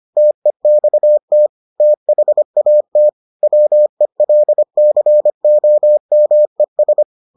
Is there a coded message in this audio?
Yes: Morse code from 0.2 to 7.0 s, heard as a keyed tone, about 25 wpm, 600 Hz. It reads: TEXT THAT WELCOMES